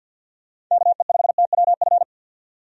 Morse code, heard as a keyed tone, about 48 wpm, 700 Hz. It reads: KE5TWF
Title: USA Call Signs - 6 Characters - 50wpm